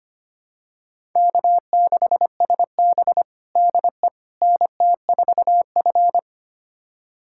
Morse code, heard as a keyed tone, about 25 wpm, 700 Hz. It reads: K6SB DE NT4F
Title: USA Callsign DE Callsign - 4 Characters - 25wpm